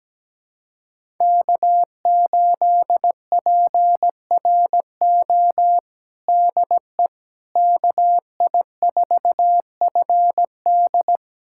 Morse code, heard as a keyed tone, about 17 wpm, 700 Hz. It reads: K8PRO DE KI4FD
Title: USA Callsign DE Callsign - 5 Characters - 17wpm